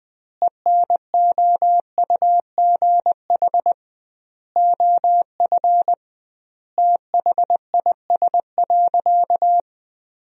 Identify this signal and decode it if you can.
Morse code, heard as a keyed tone, about 21 wpm, 700 Hz. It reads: ENOUGH OF THIS.
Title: 2x Word-Spacing - Sentences from Top 400 Words - 20wpm